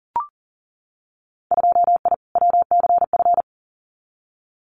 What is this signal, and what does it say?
Morse code, heard as a keyed tone, about 42 wpm, 700 Hz. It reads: 2I WCF
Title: Winter Field Day Exchanges 40wpm